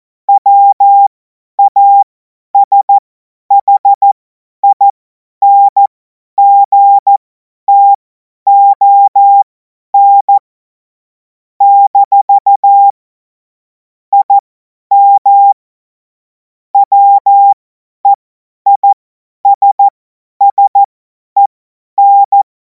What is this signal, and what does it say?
Morse code, heard as a keyed tone, about 14 wpm, spaced as for 10 wpm, 800 Hz. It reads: WASHINGTON - IM WEISSEN
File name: Morsekurs / Witze / Witz 05. Mai 2026 50 F70